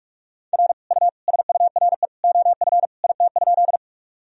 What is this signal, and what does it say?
Morse code, heard as a keyed tone, about 41 wpm, 700 Hz. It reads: R U SURE OF IT?
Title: Sentences from Top 300 Words and Abbreviations 45wpm